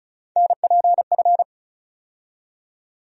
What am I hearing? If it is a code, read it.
Morse code, heard as a keyed tone, about 34 wpm, 700 Hz. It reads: NPF
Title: USA Call Sign Postfixes - 3 Letters 35wpm